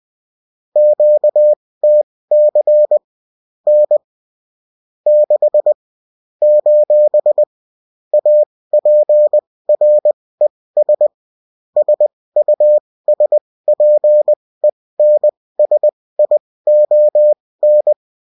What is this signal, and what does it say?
Morse code, heard as a keyed tone, about 20 wpm, spaced as for 16 wpm, 600 Hz. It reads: QTC N 6 : APRES SUSPENSION
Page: bulletin-uft-sous-forme-texte-et-en-cw